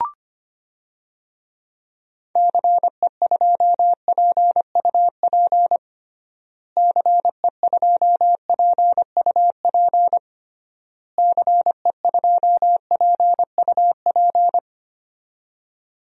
Morse code, heard as a keyed tone, about 25 wpm, 700 Hz. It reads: CE2PUP CE2PUP CE2PUP
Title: DXCC-100 Call Signs - 6 Characters - Repeated 3x - 25wpm